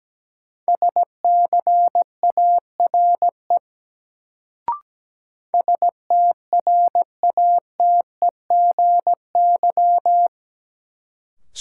Morse code, heard as a keyed tone, about 17 wpm, 700 Hz. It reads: SCARE STRATEGY
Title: Sentences from Top 1000 Words - Review Part 5 - New Word Drill - 17wpm